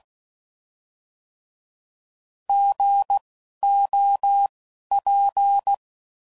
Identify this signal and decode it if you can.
Morse code, heard as a keyed tone, about 16 wpm, spaced as for 11 wpm, 800 Hz. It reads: GOP
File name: Prefixes / By First Letter